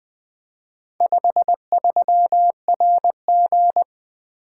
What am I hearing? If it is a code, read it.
Morse code, heard as a keyed tone, about 20 wpm, 700 Hz. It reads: 53RG
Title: Four Letter-Number Combinations - 20wpm